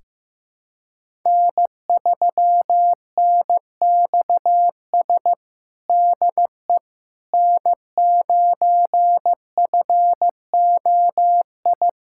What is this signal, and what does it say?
Morse code, heard as a keyed tone, about 15 wpm, 700 Hz. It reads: N3NXS DE N9FOI
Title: USA Callsign DE Callsign - 5 Characters - 15wpm